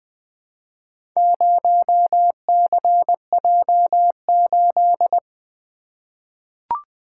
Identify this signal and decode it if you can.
Morse code, heard as a keyed tone, about 20 wpm, 700 Hz. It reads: 0CJ8
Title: Four Letter-Number Combinations - 20wpm